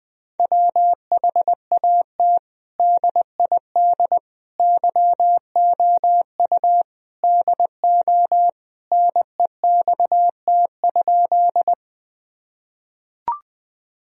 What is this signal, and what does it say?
Morse code, heard as a keyed tone, about 21 wpm, 700 Hz. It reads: WHAT DID YOU DO NEXT?
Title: Sentences from Top 1000 Words - Review Part 1 - 20wpm